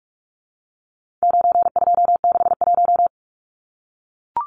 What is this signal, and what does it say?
Morse code, heard as a keyed tone, about 44 wpm, 700 Hz. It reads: 9261
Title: Numbers - 4-Digit 45wpm